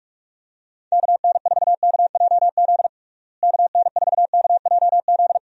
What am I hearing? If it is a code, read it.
Morse code, heard as a keyed tone, about 41 wpm, 700 Hz. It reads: KN4KJZ KN4KJZ
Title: USA Call Signs - 6 Characters - Repeated 2x - 45wpm